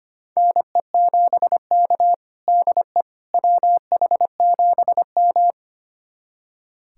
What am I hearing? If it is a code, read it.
Morse code, heard as a keyed tone, about 25 wpm, 700 Hz. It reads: NE7K DE WH7M
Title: USA Callsign DE Callsign - 4 Characters - 25wpm